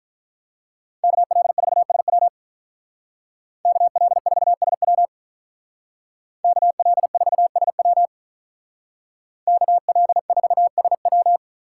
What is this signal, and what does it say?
Morse code, heard as a keyed tone, about 46 wpm, 700 Hz. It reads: KL4SW KL4SW KL4SW KL4SW